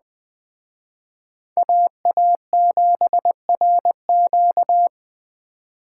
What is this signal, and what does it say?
Morse code, heard as a keyed tone, about 20 wpm, 700 Hz. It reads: AA7RQ